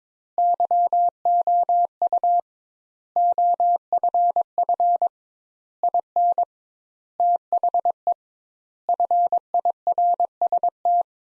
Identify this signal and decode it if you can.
Morse code, heard as a keyed tone, about 22 wpm, 700 Hz. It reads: YOU OFF IN THE FIRST
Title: Sentences Top 200 Words - 22wpm